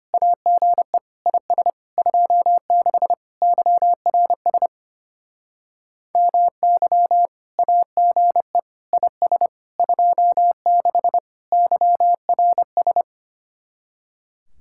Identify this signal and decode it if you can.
Morse code, heard as a keyed tone, about 30 wpm, 700 Hz. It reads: AGE IS 26 YRS MY AGE IS 26 YRS
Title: Speed-Racing Common QSO Phrases - 25wpm